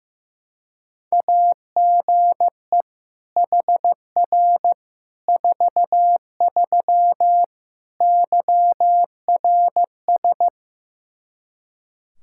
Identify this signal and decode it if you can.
Morse code, heard as a keyed tone, about 16 wpm, 700 Hz. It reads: AGE HR 43 YRS
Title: Practice QSO - Exchange 3 - More Info - 15wpm